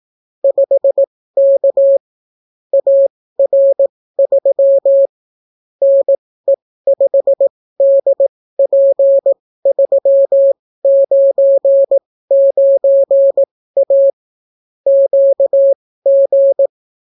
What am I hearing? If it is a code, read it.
Morse code, heard as a keyed tone, about 18 wpm, spaced as for 14 wpm, 550 Hz. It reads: 5K AR3 NE5DP399A QG